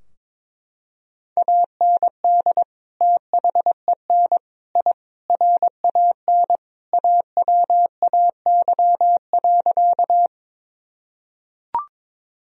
Morse code, heard as a keyed tone, about 22 wpm, 700 Hz. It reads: AND THEN I RAN AWAY.
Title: Sentences from Top 1000 Words - Review Part 3 - 22wpm